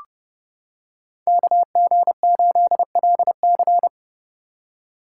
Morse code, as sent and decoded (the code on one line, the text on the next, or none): -.- --. ---.. .-.. -.-.
KG8LC